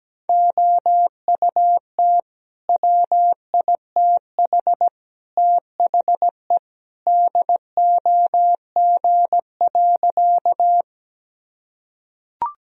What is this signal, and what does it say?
Morse code, heard as a keyed tone, about 17 wpm, 700 Hz. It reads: OUT WITH THE DOG.
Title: Sentences from Top 1000 Words - Review Part 2 - 17wpm